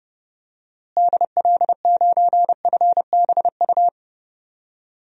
Morse code, heard as a keyed tone, about 31 wpm, 700 Hz. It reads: DL9FBU
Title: DXCC-100 Call Signs - 6 Characters - Repeated 3x - 30wpm